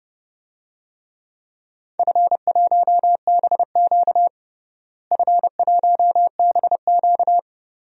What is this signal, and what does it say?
Morse code, heard as a keyed tone, about 30 wpm, 700 Hz. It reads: F1BQ F1BQ
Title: DXCC-100 Call Signs - 4 Characters - Repeated 2x - 30wpm